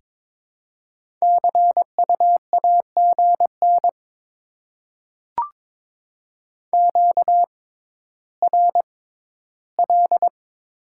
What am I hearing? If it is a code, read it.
Morse code, heard as a keyed tone, about 22 wpm, spaced as for 13 wpm, 700 Hz. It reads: CU AGN QRL
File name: Top 20 QSO Elements - 6x Character Spacing - 22wpm